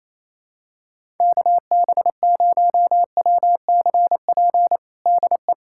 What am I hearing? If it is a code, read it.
Morse code, heard as a keyed tone, about 28 wpm, 700 Hz. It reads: KB0WCP DE
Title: USA Callsign DE Callsign - 6 Characters - 28wpm